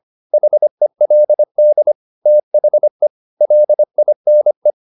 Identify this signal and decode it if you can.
Morse code, heard as a keyed tone, about 25 wpm, 600 Hz. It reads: HELD THE LINE
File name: the old man and the sea part 1 25wpm